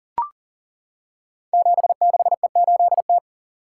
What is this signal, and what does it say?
Morse code, heard as a keyed tone, about 40 wpm, 700 Hz. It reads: 76E8T
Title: Code Groups - 5 Letter:Numbers - 40wpm